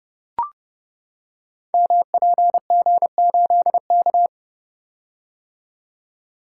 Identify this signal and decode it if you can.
Morse code, heard as a keyed tone, about 30 wpm, 700 Hz. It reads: MPG8K
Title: Code Groups - 5 Letter:Numbers - 30wpm